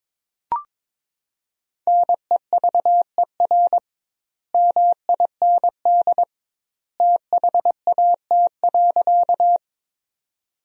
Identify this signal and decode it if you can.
Morse code, heard as a keyed tone, about 22 wpm, 700 Hz. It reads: NEVER MIND THAT.
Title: 2x Word-Spacing - Sentences from Top 500 Words - 22wpm